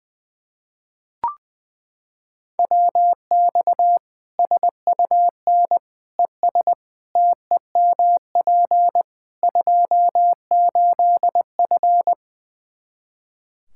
Morse code, heard as a keyed tone, about 20 wpm, 700 Hz. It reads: WX SUN ES TEMP 28F
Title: Practice QSO - Exchange 2 - Info - 20wpm